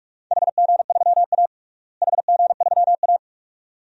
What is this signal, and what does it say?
Morse code, heard as a keyed tone, about 46 wpm, 700 Hz. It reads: HG3A HG3A
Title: DXCC-100 Call Signs - 4 Characters - Repeated 2x - 45wpm